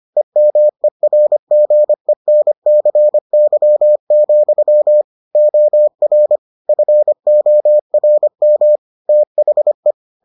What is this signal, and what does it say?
Morse code, heard as a keyed tone, about 25 wpm, 600 Hz. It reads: EMERGENCY, OR FORM THE